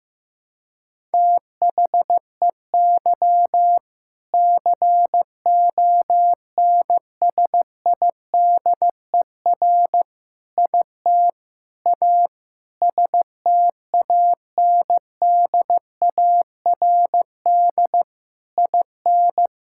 Morse code, heard as a keyed tone, about 16 wpm, 700 Hz. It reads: THEY CONSIDER IT A STANDARD IN
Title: Sentences from Top 1000 Words - Review Part 6 - 15wpm